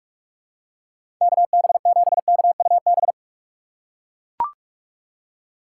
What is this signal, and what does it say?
Morse code, heard as a keyed tone, about 41 wpm, 700 Hz. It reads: KB7KUB
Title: USA Call Signs - 6 Characters - 4x Character Spacing - 45wpm